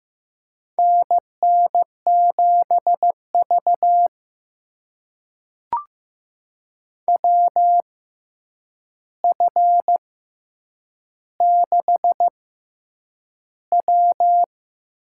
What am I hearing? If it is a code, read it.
Morse code, heard as a keyed tone, about 15 wpm, spaced as for 8 wpm, 700 Hz. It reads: NN7V WF6W
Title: USA Call Signs - 4 Characters - 6x Character Spacing - 15wpm